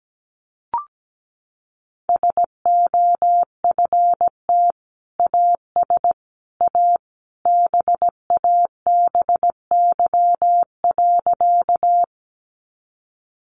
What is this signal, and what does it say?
Morse code, heard as a keyed tone, about 17 wpm, 700 Hz. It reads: SOFT AS A BABY.
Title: Sentences from Top 1500 Words 17wpm